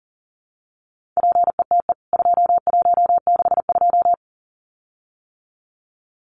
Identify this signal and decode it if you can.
Morse code, heard as a keyed tone, about 40 wpm, 700 Hz. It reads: PETE 2162
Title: CWT Exchange 40wpm